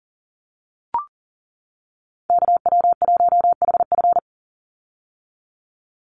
Morse code, heard as a keyed tone, about 40 wpm, 700 Hz. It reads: KW1HF
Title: USA Call Signs 40wpm